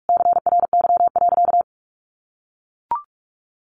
Morse code, heard as a keyed tone, about 44 wpm, 700 Hz. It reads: CRY.
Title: Sentences from Top 1400 Words 45wpm